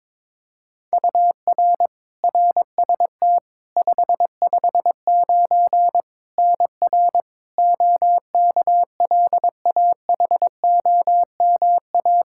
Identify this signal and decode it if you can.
Morse code, heard as a keyed tone, about 22 wpm, 700 Hz. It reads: UR RST 559 NR OKLAHOMA
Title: Practice QSO - Exchange 1 - Beginning - 22wpm